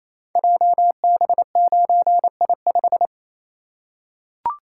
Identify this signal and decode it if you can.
Morse code, heard as a keyed tone, about 27 wpm, 700 Hz. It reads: JB9I5
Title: Code Groups - 5 Letter:Numbers - 28wpm